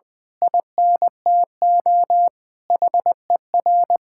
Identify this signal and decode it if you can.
Morse code, heard as a keyed tone, about 20 wpm, 700 Hz. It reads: INTO HER
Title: Sentences from Top 1000 Words - Review Part 6 - 20wpm